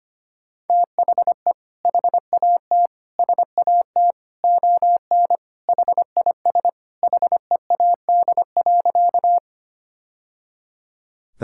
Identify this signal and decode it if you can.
Morse code, heard as a keyed tone, about 25 wpm, 700 Hz. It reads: THE HAT SAT ON HIS HEAD.
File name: practice / 25wpm / Sentences with 2-Word Phrases - 25wpm